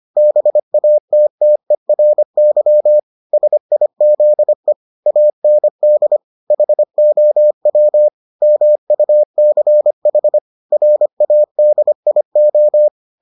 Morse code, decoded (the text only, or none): BATTERY SIZE AND HOW MUCH RADIO